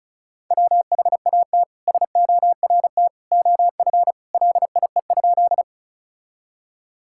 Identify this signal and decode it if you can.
Morse code, heard as a keyed tone, about 34 wpm, 700 Hz. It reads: WHAT SORT OF LIE?
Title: Sentences from Top 1600 Words 35wpm